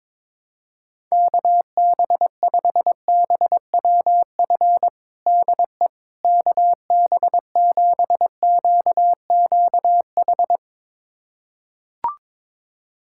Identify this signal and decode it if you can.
Morse code, heard as a keyed tone, about 22 wpm, 700 Hz. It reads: KB5BWF DE KB7QQH
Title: USA Callsign DE Callsign 22wpm